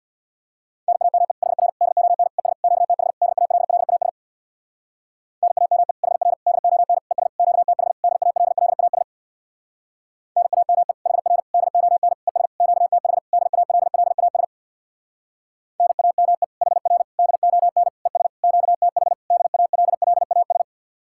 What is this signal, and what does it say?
Morse code, heard as a keyed tone, about 64 wpm, 700 Hz. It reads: NAME HR DON ES QTH DALLAS NAME HR DON ES QTH DALLAS NAME HR DON ES QTH DALLAS NAME HR DON ES QTH DALLAS